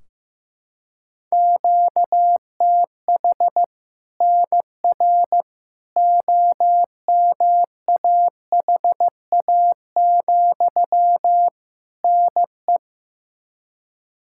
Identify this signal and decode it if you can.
Morse code, heard as a keyed tone, about 15 wpm, 700 Hz. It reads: QTH NR OMAHA, NE
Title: Practice QSO - Exchange 1 - Beginning - 15wpm